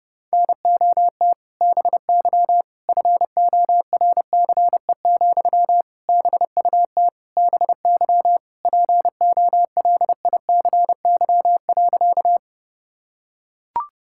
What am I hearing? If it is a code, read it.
Morse code, heard as a keyed tone, about 30 wpm, 700 Hz. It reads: NOT BY FORCE, BUT BY POLICY.
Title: Sentences from Top 1000 Words - Review Part 5 - 30wpm